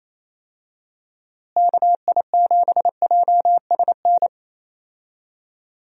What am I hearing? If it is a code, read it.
Morse code, heard as a keyed tone, about 28 wpm, 700 Hz. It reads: KI7JSN
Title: USA Call Signs - 6 Characters - Repeated 3x - 28wpm